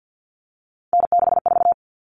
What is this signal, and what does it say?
Morse code, heard as a keyed tone, about 50 wpm, 700 Hz. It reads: N64
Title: Common Cut Numbers - 3-Digit 50wpm